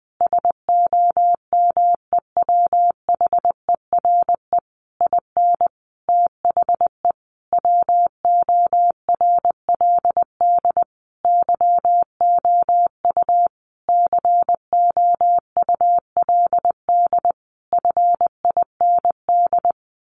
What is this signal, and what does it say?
Morse code, heard as a keyed tone, about 20 wpm, 700 Hz. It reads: SOMEWHERE IN THE WORLD YOU COULD FIND
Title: Sentences from Top 1400 Words 20wpm